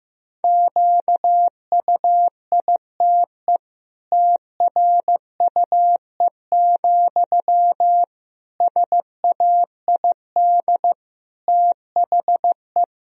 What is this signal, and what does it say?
Morse code, heard as a keyed tone, about 15 wpm, 700 Hz. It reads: QUITE TRUE, SAID THE
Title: Sentences from Top 1000 Words - Review Part 3 - 15wpm